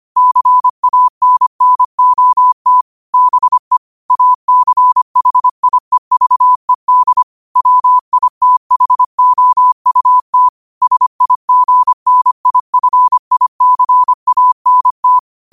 Morse code, heard as a keyed tone, about 25 wpm, 1000 Hz. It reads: CANNOT BE ACHIEVED WITHOUT SIGNIFICANT
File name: qotdfiles / 25wpm